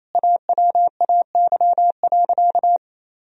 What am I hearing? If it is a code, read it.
Morse code, heard as a keyed tone, about 28 wpm, 700 Hz. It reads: AWAY.
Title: Sentences from Top 1000 Words - Review Part 1 - 28wpm